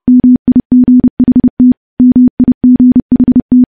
Morse code, heard as a keyed tone, about 30 wpm, 250 Hz. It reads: MIGHT MIGHT
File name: morse / morse-training-30WPM